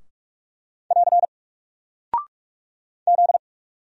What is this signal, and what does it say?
Morse code, heard as a keyed tone, about 46 wpm, 700 Hz. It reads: <AR> Z
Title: Single Character (Letter:Number:Punctuation:Prosign) 45wpm